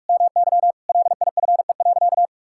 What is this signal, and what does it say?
Morse code, heard as a keyed tone, about 46 wpm, 700 Hz. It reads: MY LIFE.
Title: Sentences from Top 1000 Words - Review Part 5 - 45wpm